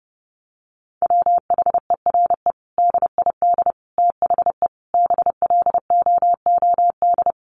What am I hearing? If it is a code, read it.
Morse code, heard as a keyed tone, about 30 wpm, 700 Hz. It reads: WHERE DID THE BLOOD
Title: Sentences from Top 1100 Words 30wpm